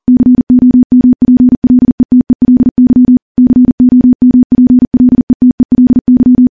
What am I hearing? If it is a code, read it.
Morse code, heard as a keyed tone, about 40 wpm, 250 Hz. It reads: COMPLETELY COMPLETELY
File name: morse / morse-training-40WPM